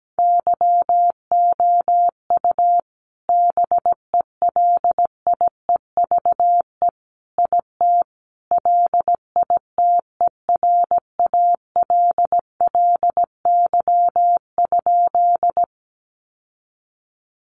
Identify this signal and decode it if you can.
Morse code, heard as a keyed tone, about 17 wpm, 700 Hz. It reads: YOU BELIEVE IT LITERALLY?
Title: Sentences from Top 1400 Words 17wpm